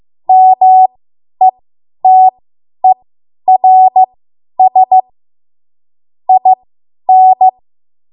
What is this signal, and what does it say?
Morse code, heard as a keyed tone, about 15 wpm, spaced as for 10 wpm, 750 Hz. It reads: METERS IN
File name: Morse / 10WPM